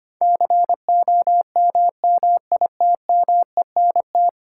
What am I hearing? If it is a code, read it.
Morse code, heard as a keyed tone, about 25 wpm, 700 Hz. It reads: COMMITMENT
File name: Ten Letter Words - 25wpm